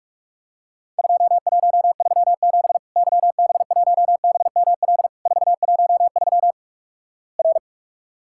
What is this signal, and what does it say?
Morse code, heard as a keyed tone, about 46 wpm, 700 Hz. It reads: 2137 YB1BML 413 R
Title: Worked All Europe DX Contest - QTC Exchange - 45wpm